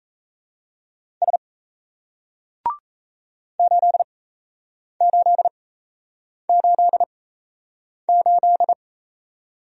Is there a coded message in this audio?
Yes: Morse code from 1.2 to 8.7 s, heard as a keyed tone, about 38 wpm, spaced as for 8 wpm, 700 Hz. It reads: S 8888